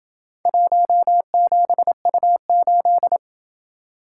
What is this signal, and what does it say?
Morse code, heard as a keyed tone, about 28 wpm, 700 Hz. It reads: 17U8